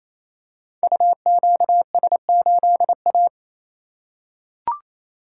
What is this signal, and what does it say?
Morse code, heard as a keyed tone, about 27 wpm, 700 Hz. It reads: UQS8A
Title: Code Groups - 5 Letter:Numbers - 28wpm